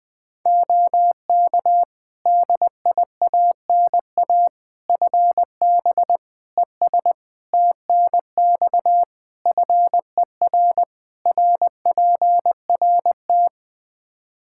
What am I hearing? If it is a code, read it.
Morse code, heard as a keyed tone, about 21 wpm, 700 Hz. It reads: OK DIANA FB ES TNX FER RPRT
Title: Practice QSO - Exchange 2 - Info - 20wpm